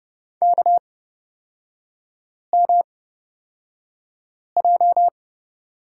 Morse code, heard as a keyed tone, about 31 wpm, spaced as for 5 wpm, 690 Hz. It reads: KMJ
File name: Lesson 38e - Letter q - Three Characters 30wpm